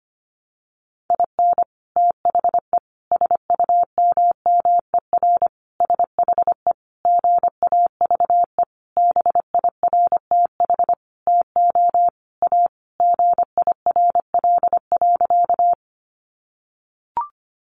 Morse code, heard as a keyed tone, about 25 wpm, 700 Hz. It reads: IN THE SUMMER SHE GAVE BIRTH TO A GIRL.
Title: Sentences from Top 1100 Words 25wpm